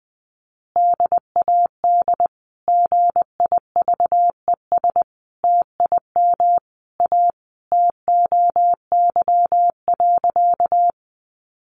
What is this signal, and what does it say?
Morse code, heard as a keyed tone, about 20 wpm, 700 Hz. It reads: DAD GIVES TIM A TOY.
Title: Easy Phrases 20wpm